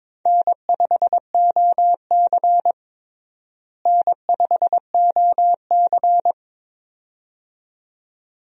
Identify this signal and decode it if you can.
Morse code, heard as a keyed tone, about 22 wpm, 700 Hz. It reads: N5OC N5OC
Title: USA Call Signs - 4 Characters - Repeated 2x - 22wpm